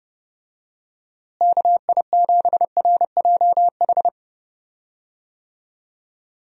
Morse code, heard as a keyed tone, about 30 wpm, 700 Hz. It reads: KI7RJH
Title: USA Call Signs - 6 Characters - Repeated 3x - 30wpm